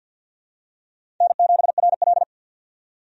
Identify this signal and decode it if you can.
Morse code, heard as a keyed tone, about 53 wpm, 700 Hz. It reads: N7RL
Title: USA Call Signs - 4 Characters - 50wpm